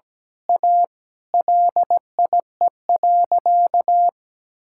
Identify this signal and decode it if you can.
Morse code, heard as a keyed tone, about 17 wpm, 700 Hz. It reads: A LIE.
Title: Sentences from Top 1600 Words 17wpm